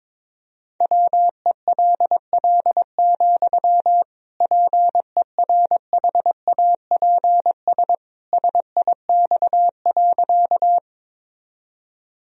Morse code, heard as a keyed tone, about 22 wpm, 700 Hz. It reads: WELL, PERHAPS SIX.
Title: Sentences from Top 1000 Words - Review Part 2 - 22wpm